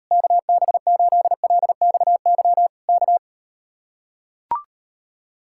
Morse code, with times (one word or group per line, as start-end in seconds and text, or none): 0.1-2.7 KB8LXY
2.9-3.2 K